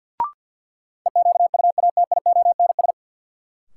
Morse code, heard as a keyed tone, about 53 wpm, 700 Hz. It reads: EQUATIONS
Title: Sets of 5 Words - New Word Drill - 50wpm